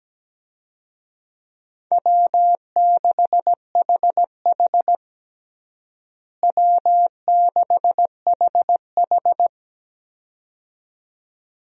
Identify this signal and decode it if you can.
Morse code, heard as a keyed tone, about 17 wpm, 700 Hz. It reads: W6HH W6HH
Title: USA Call Signs - 4 Characters - Repeated 2x - 17wpm